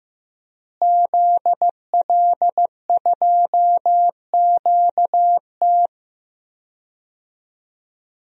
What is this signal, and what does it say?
Morse code, heard as a keyed tone, about 15 wpm, 700 Hz. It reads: ZL2QT
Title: DXCC-100 Call Signs - 5 Characters - Repeated 3x - 15wpm